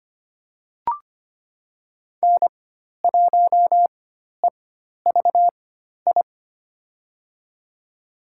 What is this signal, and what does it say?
Morse code, heard as a keyed tone, about 25 wpm, spaced as for 12 wpm, 700 Hz. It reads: N1EVI
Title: USA Call Signs - 5 Characters - 4x Character Spacing - 25wpm